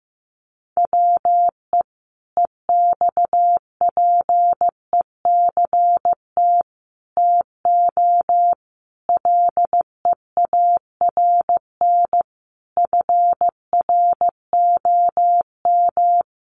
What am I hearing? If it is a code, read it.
Morse code, heard as a keyed tone, about 15 wpm, 700 Hz. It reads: WE EXPECT TO LEARN FROM
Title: Sentences from Top 1100 Words 15wpm